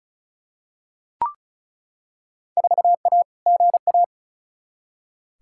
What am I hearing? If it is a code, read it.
Morse code, heard as a keyed tone, about 36 wpm, 700 Hz. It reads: 4A GA
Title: Field Day Exchanges - 35wpm